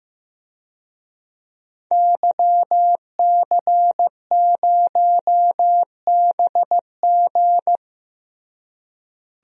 Morse code, heard as a keyed tone, about 15 wpm, 700 Hz. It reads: YC0BG
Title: DXCC-100 Call Signs - 5 Characters - Repeated 2x - 15wpm